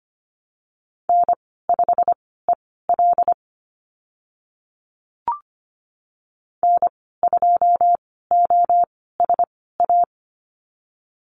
Morse code, heard as a keyed tone, about 25 wpm, spaced as for 16 wpm, 690 Hz. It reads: N5EL N2OSA
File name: Lesson 15g - Number 5 - Call Signs 25wpm